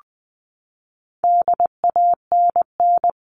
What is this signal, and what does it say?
Morse code, heard as a keyed tone, about 20 wpm, 700 Hz. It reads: DANN
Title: All Cut Numbers - 4-Digit 20wpm